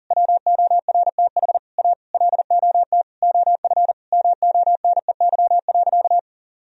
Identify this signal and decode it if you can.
Morse code, heard as a keyed tone, about 37 wpm, 700 Hz. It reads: WORTH A LOT OF MONEY.
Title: Sentences from Top 1000 Words - Review Part 4 - 40wpm